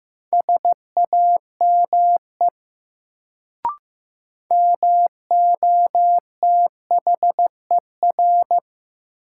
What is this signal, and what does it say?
Morse code, heard as a keyed tone, about 15 wpm, 700 Hz. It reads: SAME MOTHER
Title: Sentences from Top 200 Words - New Word Drill - 15wpm